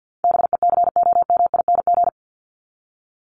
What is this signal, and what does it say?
Morse code, heard as a keyed tone, about 50 wpm, 700 Hz. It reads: BECOMING
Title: ING Suffix - Top 100 Words with Base Words 50wpm